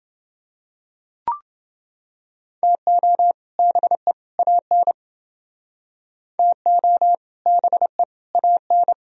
Morse code, heard as a keyed tone, about 29 wpm, 700 Hz. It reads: TO BE AN TO BE AN